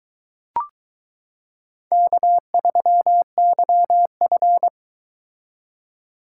Morse code, heard as a keyed tone, about 22 wpm, 700 Hz. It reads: K3YF